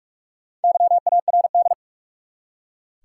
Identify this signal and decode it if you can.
Morse code, heard as a keyed tone, about 47 wpm, 700 Hz. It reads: YARD
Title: Four Letter Words - 45wpm